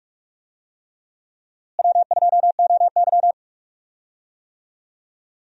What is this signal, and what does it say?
Morse code, heard as a keyed tone, about 42 wpm, 700 Hz. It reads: W2OY
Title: USA Call Signs - 4 Characters - Repeated 3x - 45wpm